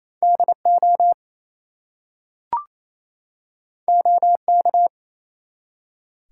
Morse code, heard as a keyed tone, about 27 wpm, 700 Hz. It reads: DO OK
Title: Two Letter Words - 28wpm